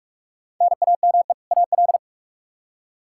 Morse code, heard as a keyed tone, about 44 wpm, 700 Hz. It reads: NAME AL
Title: Practice QSO - Exchange 1 - Beginning - 45wpm